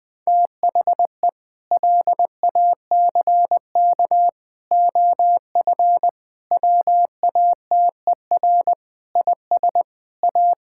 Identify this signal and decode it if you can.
Morse code, heard as a keyed tone, about 20 wpm, 700 Hz. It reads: THE LACK OF WATER IS A
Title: Sentences with 3-Word Phrases - 20wpm